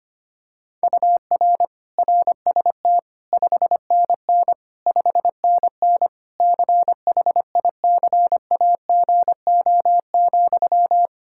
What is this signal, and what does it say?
Morse code, heard as a keyed tone, about 25 wpm, 700 Hz. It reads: UR RST 5NN 5NN CHICAGO,
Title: K1USN SST QSO Practice 25wpm